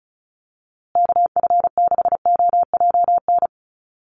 Morse code, heard as a keyed tone, about 34 wpm, 700 Hz. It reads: KF6OJN
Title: USA Call Signs Repeated 2x 35wpm